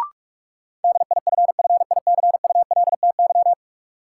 Morse code, heard as a keyed tone, about 46 wpm, 700 Hz. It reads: DIFFICULTY
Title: Sentences from Top 1500 Words - New Word Drill Alt - 45wpm